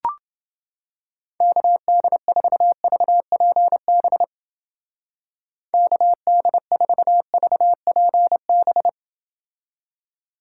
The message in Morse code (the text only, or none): KD4VPB KD4VPB